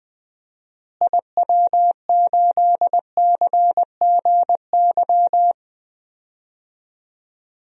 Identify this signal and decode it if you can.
Morse code, heard as a keyed tone, about 20 wpm, 700 Hz. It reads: IW8CGY
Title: DXCC-100 Call Signs - 6 Characters - Repeated 3x - 20wpm